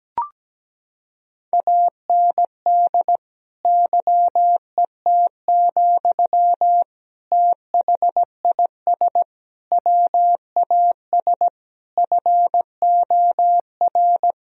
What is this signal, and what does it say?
Morse code, heard as a keyed tone, about 17 wpm, 700 Hz. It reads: AND YET, THIS WAS FOR
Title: Sentences from Top 1000 Words - Review Part 3 - 17wpm